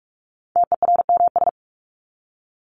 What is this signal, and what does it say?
Morse code, heard as a keyed tone, about 44 wpm, 700 Hz. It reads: TERMS
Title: ES:S Suffix - Top 100 Words with Base Words 45wpm